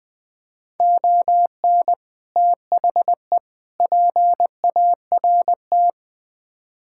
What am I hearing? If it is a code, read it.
Morse code, heard as a keyed tone, about 21 wpm, 700 Hz. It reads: ON THE PART
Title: Build A Phrase - 4-Word Phrases - 20wpm